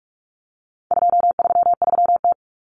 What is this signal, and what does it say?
Morse code, heard as a keyed tone, about 44 wpm, 700 Hz. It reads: 233T
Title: Common Cut Numbers - 4-Digit 45wpm